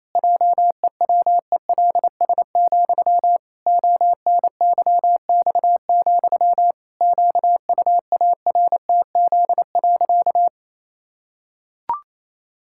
Morse code, heard as a keyed tone, about 28 wpm, 700 Hz. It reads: JEWELS, ONYX, QUARTZ.
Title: Pangram Challenge v2 - 28wpm